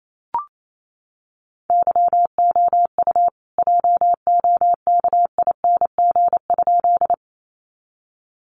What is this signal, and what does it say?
Morse code, heard as a keyed tone, about 28 wpm, 700 Hz. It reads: YOU JOKING?
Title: A Sound Of Thunder 28wpm